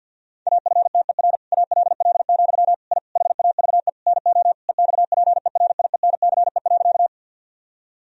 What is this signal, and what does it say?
Morse code, heard as a keyed tone, about 50 wpm, 700 Hz. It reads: AFTER ALL, I HAVE NO EXPERIENCE.
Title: Sentences from Top 1000 Words - Review Part 3 - 50wpm